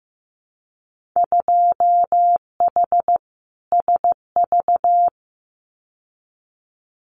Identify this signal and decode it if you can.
Morse code, heard as a keyed tone, about 15 wpm, 700 Hz. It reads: 2H SV
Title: Winter Field Day Exchanges Encore 15wpm